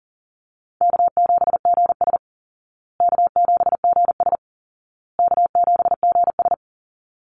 Morse code, heard as a keyed tone, about 38 wpm, 700 Hz. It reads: K7GS K7GS K7GS
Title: Easy Call Signs Repeated 3x - 40wpm